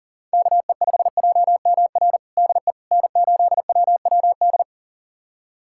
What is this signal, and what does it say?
Morse code, heard as a keyed tone, about 40 wpm, 700 Hz. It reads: KE5JMR DE N8WWD
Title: USA Callsign DE Callsign 40wpm